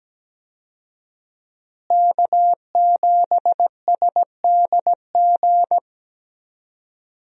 Morse code, heard as a keyed tone, about 17 wpm, 700 Hz. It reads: K7SDG
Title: USA Call Signs - 5 Characters - Repeated 3x - 17wpm